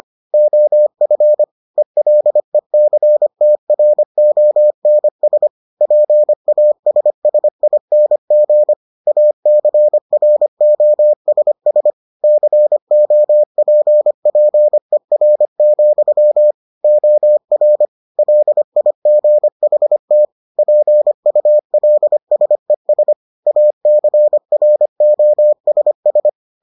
Morse code, heard as a keyed tone, about 25 wpm, 600 Hz. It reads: OF ELECTRONS PASSING ACROSS COPPER, OR LIGHT PULSES ACROSS